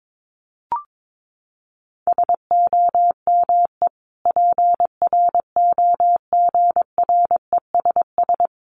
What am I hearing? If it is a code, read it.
Morse code, heard as a keyed tone, about 22 wpm, 700 Hz. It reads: SOME PROGRESS
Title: Sentences from Top 1400 Words 22wpm